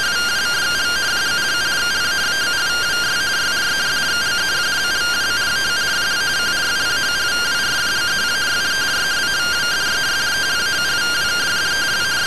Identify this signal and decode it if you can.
Two-tone teleprinter signal, about 170 Hz shift.
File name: findings / MR AXIAL 33,99mm 1.gen-03 rtty rx